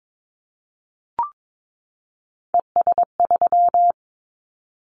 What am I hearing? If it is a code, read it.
Morse code, heard as a keyed tone, about 22 wpm, 700 Hz. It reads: ES3
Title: Callsign Prefixes for Top 50 DXCC - 22wpm